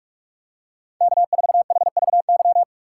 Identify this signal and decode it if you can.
Morse code, heard as a keyed tone, about 46 wpm, 700 Hz. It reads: K4HVY
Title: USA Call Signs - 5 Characters - 6x Character Spacing - 45wpm